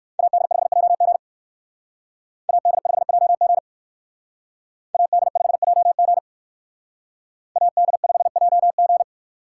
Morse code, heard as a keyed tone, about 58 wpm, 690 Hz. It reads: AD5JG AD5JG AD5JG AD5JG